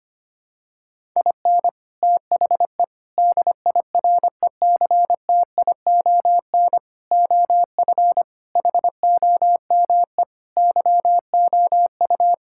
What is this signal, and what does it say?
Morse code, heard as a keyed tone, about 25 wpm, 700 Hz. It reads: IN THE DIRECTION OF HOME YOU
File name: practice / 25wpm / Sentences with 4-Word Phrases - 25wpm